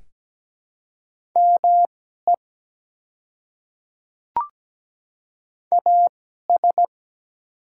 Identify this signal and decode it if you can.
Morse code, heard as a keyed tone, about 17 wpm, spaced as for 12 wpm, 700 Hz. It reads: ME AS
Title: 2x Character Spacing - Top 200 Words - 17wpm